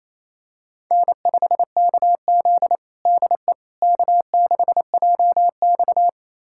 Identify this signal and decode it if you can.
Morse code, heard as a keyed tone, about 28 wpm, 700 Hz. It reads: N5KZ DE K6JX
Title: USA Callsign DE Callsign - 4 Characters - 28wpm